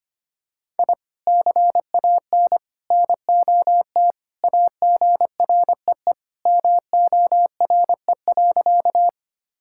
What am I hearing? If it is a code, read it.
Morse code, heard as a keyed tone, about 25 wpm, 700 Hz. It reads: I CAN NOT AGREE MORE.
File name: practice / 25wpm / Sentences with 3-Word Phrases - 25wpm